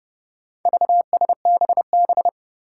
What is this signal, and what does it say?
Morse code, heard as a keyed tone, about 30 wpm, 700 Hz. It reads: VSBB
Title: Four Challenging Letter-Number Combinations 30wpm